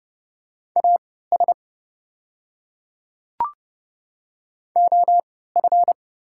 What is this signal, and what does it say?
Morse code, heard as a keyed tone, about 30 wpm, spaced as for 17 wpm, 700 Hz. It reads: AS OF
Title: 3x Character Spacing - Top 100 Words - 30wpm